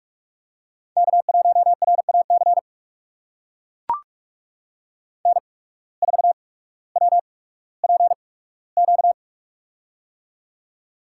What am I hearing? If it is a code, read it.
Morse code, heard as a keyed tone, about 43 wpm, spaced as for 19 wpm, 700 Hz. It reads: K1RAC N4WPQ
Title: USA Call Signs - 5 Characters - 8x Character Spacing - 45wpm